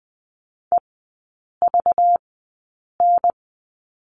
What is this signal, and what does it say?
Morse code, heard as a keyed tone, about 19 wpm, spaced as for 8 wpm, 690 Hz. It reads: EVN
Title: 205 Lesson 33e - Letter j - Three Characters 20wpm